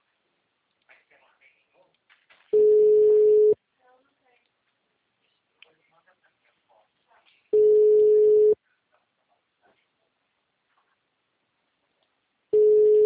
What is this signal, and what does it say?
Ringback tone (Europe, 425 Hz), with speech.
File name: out- -2025- - - .655